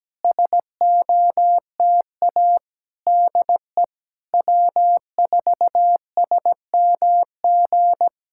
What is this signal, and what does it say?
Morse code, heard as a keyed tone, about 17 wpm, 700 Hz. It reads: SOTA DE W4SMG
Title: SOTA QSO Practice 17wpm